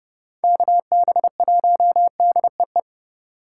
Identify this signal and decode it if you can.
Morse code, heard as a keyed tone, about 30 wpm, 700 Hz. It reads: KB1DEE